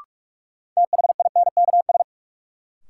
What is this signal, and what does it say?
Morse code, heard as a keyed tone, about 46 wpm, 700 Hz. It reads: THINKS
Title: Sentences from Top 1000 Words - Review Part 6 - New Word Drill - 45wpm